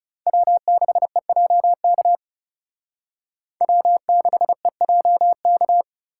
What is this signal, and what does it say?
Morse code, heard as a keyed tone, about 36 wpm, 700 Hz. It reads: W6EJK W6EJK
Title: Speed Racing - USA 1x3 Call Signs 30wpm